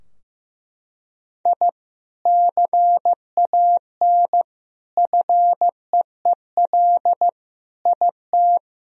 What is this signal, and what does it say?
Morse code, heard as a keyed tone, about 15 wpm, 700 Hz. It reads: I CAN FEEL IT
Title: Sentences from Top 1000 Words - Review Part 2 - 15wpm